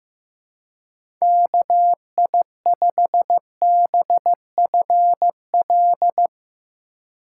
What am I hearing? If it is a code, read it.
Morse code, heard as a keyed tone, about 17 wpm, 700 Hz. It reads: KI5BFL
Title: Speed Racing - USA Call Signs 15wpm